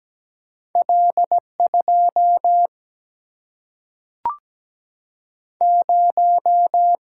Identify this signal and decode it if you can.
Morse code, heard as a keyed tone, about 17 wpm, 700 Hz. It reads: L 2 0
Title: Single-Letter Single-Number 17wpm